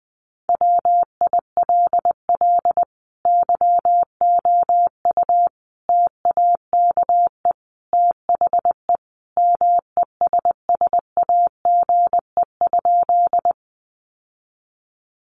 Morse code, heard as a keyed tone, about 20 wpm, 700 Hz. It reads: WILL YOU TAKE THE MESSAGE?
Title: Sentences from Top 1100 Words 20wpm